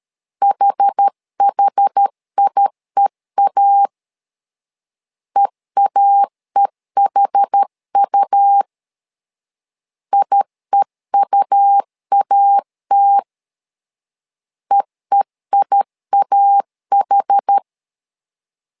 Morse code, heard as a keyed tone, about 13 wpm, 800 Hz. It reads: HHIEA EAEHU IEUAT EEIAH
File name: LEZIONI / 2 UHDN / 2 UH 12wpm 1439